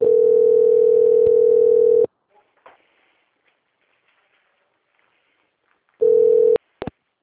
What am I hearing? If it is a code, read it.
Ringback tone (North America), 440 + 480 Hz.
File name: out- - - - - .19580